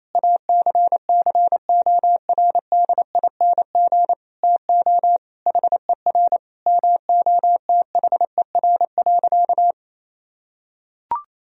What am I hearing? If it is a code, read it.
Morse code, heard as a keyed tone, about 28 wpm, 700 Hz. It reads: ACCORDING TO HER MOTHER.
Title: Sentences from Top 1000 Words - Review Part 4 - 28wpm